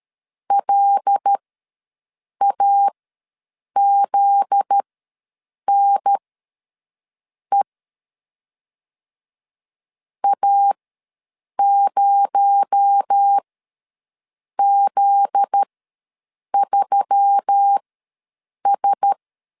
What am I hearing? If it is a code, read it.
Morse code, heard as a keyed tone, about 13 wpm, spaced as for 6 wpm, 800 Hz. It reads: LAZNE A0Z3S
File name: LEZIONI / 7 GF9Z / 7 GF9Z ws 12wpm 3321